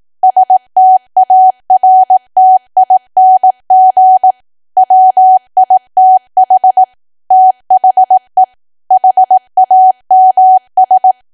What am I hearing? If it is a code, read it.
Morse code, heard as a keyed tone, about 18 wpm, 750 Hz. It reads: STARTING WITH THE HAMS
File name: Archive / 18 WPM / 18WPM